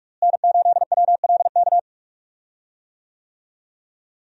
Morse code, heard as a keyed tone, about 45 wpm, 700 Hz. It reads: N8WLK
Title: USA Call Signs - 5 Characters - Repeated 3x - 45wpm